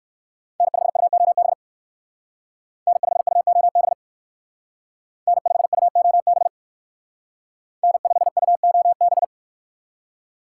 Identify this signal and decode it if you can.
Morse code, heard as a keyed tone, about 59 wpm, 700 Hz. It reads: N5UOB N5UOB N5UOB N5UOB